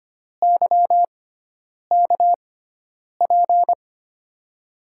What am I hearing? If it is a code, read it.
Morse code, heard as a keyed tone, about 25 wpm, spaced as for 9 wpm, 700 Hz. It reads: YKP
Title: USA Call Signs - 6x Character Spacing - 25wpm